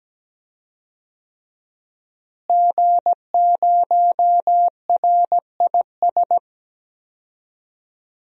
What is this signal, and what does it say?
Morse code, heard as a keyed tone, about 17 wpm, 700 Hz. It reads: G0RIS
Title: DXCC-100 Call Signs - 5 Characters - Repeated 3x - 17wpm